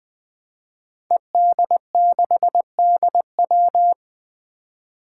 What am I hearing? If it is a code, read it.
Morse code, heard as a keyed tone, about 20 wpm, 700 Hz. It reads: ED6DW
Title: Five Challenging Letter-Number Combinations 20wpm